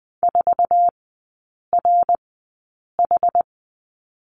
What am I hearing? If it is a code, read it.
Morse code, heard as a keyed tone, about 20 wpm, spaced as for 8 wpm, 700 Hz. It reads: 4RH
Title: 060 Lesson 11e - Letter h - Three Characters 20wpm